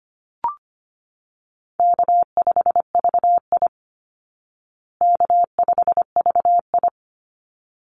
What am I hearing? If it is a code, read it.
Morse code, heard as a keyed tone, about 25 wpm, 700 Hz. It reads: K5VI K5VI
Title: Easy Call Signs Repeated 2x - 25wpm